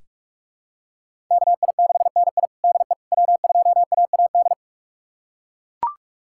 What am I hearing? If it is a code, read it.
Morse code, heard as a keyed tone, about 42 wpm, 700 Hz. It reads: KI6NI DE W2AAD
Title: USA Callsign DE Callsign - 5 Characters - 45wpm